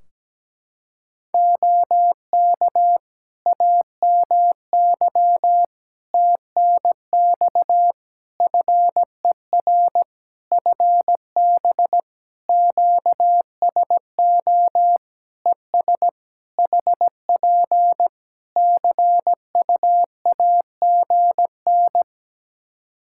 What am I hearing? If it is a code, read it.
Morse code, heard as a keyed tone, about 17 wpm, 700 Hz. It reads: OK AMY TNX FER FB QSO ES HP CUAGN
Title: Practice QSO - Exchange 4 - Ending - 17wpm